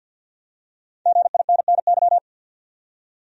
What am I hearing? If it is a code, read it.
Morse code, heard as a keyed tone, about 52 wpm, 700 Hz. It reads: GINNY
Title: USA Common Names - Max 5 Letters - 50wpm